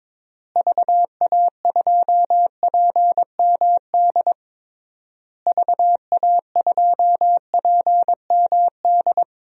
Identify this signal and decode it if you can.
Morse code, heard as a keyed tone, about 22 wpm, 700 Hz. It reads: VA2PMD VA2PMD
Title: DXCC-100 Call Signs - 6 Characters - Repeated 3x - 22wpm